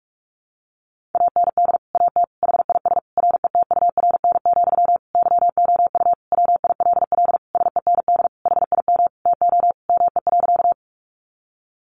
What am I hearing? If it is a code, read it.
Morse code, heard as a keyed tone, about 44 wpm, 700 Hz. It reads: AND AT HIS RETURN, YOU WILL SEND HIM TO ME.
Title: Sentences from Top 1500 Words 45wpm